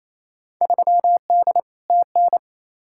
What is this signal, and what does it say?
Morse code, heard as a keyed tone, about 28 wpm, 700 Hz. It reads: 3D TN
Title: Field Day Exchanges Encore - 28wpm